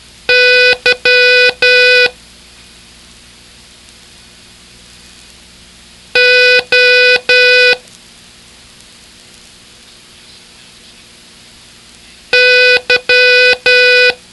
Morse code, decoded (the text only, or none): Y O Y